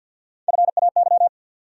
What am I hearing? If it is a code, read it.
Morse code, heard as a keyed tone, about 47 wpm, 700 Hz. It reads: FAY
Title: USA Call Sign Postfixes - 3 Letters 50wpm